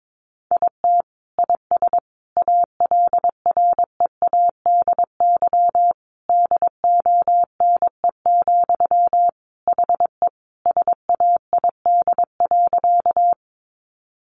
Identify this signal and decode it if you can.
Morse code, heard as a keyed tone, about 22 wpm, 700 Hz. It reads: IT IS ALREADY DONE, HE SAID.
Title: Sentences from Top 700 Words 22wpm